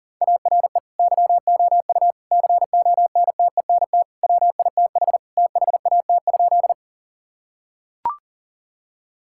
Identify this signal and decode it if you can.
Morse code, heard as a keyed tone, about 40 wpm, 700 Hz. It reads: ARE YOU CONTENT WITH THAT?
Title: Sentences from Top 1000 Words - Review Part 6 - 40wpm